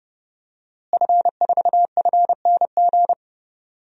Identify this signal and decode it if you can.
Morse code, heard as a keyed tone, about 30 wpm, 700 Hz. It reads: F4FNG
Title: DXCC-100 Call Signs - 5 Characters - Repeated 2x - 30wpm